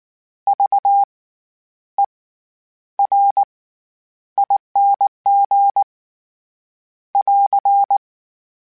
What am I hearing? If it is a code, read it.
Morse code, heard as a keyed tone, about 19 wpm, spaced as for 10 wpm, 800 Hz. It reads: VERING <AR>
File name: Special INGs / Special INGs-mp3s / SlowToFaster ing @18